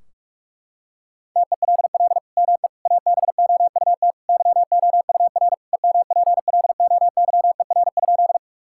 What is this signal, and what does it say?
Morse code, heard as a keyed tone, about 46 wpm, 700 Hz. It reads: TELL ME ABOUT YOUR EMPLOYER?
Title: Sentences from Top 1600 Words 45wpm